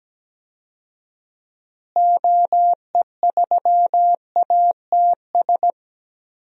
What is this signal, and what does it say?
Morse code, heard as a keyed tone, about 17 wpm, 700 Hz. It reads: OE3ATS
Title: DXCC-100 Call Signs - 6 Characters - Repeated 3x - 17wpm